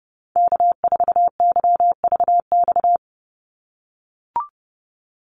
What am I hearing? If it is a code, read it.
Morse code, heard as a keyed tone, about 30 wpm, 700 Hz. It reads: K4YVX
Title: USA Call Signs 30wpm